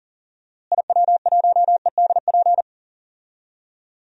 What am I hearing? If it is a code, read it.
Morse code, heard as a keyed tone, about 40 wpm, 700 Hz. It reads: IW1EDP
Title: DXCC-100 Call Signs - 6 Characters - Repeated 3x - 40wpm